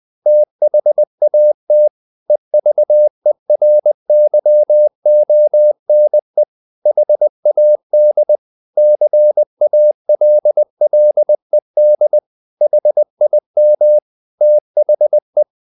Morse code, heard as a keyed tone, about 20 wpm, 600 Hz. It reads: THAT EVERYONE HAD CALLED HIM THE